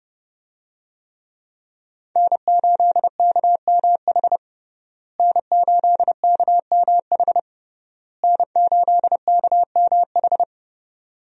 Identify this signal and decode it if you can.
Morse code, heard as a keyed tone, about 30 wpm, 700 Hz. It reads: N8KMH N8KMH N8KMH
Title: USA Call Signs - 5 Characters - Repeated 3x - 30wpm